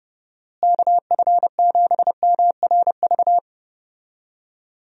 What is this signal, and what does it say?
Morse code, heard as a keyed tone, about 31 wpm, 700 Hz. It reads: KF7MRV
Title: USA Call Signs - 6 Characters - Repeated 2x - 30wpm